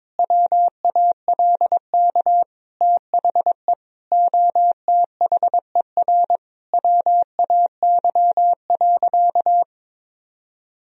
Morse code, heard as a keyed tone, about 22 wpm, 700 Hz. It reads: WALK THE OTHER WAY.
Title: Sentences from Top 1000 Words - Review Part 1 - 22wpm